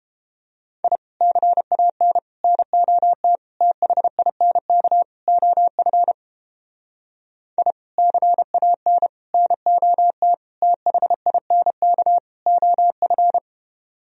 Morse code, heard as a keyed tone, about 29 wpm, 700 Hz. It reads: I CAN NOT THINK OF I CAN NOT THINK OF